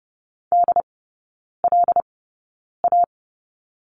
Morse code, heard as a keyed tone, about 30 wpm, spaced as for 9 wpm, 690 Hz. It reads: DLA
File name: Lesson 17e - Letter u - Three Characters 30wpm